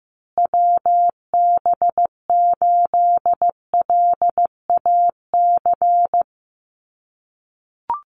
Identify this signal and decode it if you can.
Morse code, heard as a keyed tone, about 15 wpm, 700 Hz. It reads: WB8LAC
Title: USA Call Signs Repeated 2x 15wpm